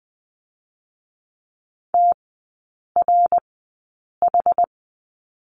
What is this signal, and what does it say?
Morse code, heard as a keyed tone, about 20 wpm, spaced as for 8 wpm, 700 Hz. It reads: TRH
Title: 060 Lesson 11e - Letter h - Three Characters 20wpm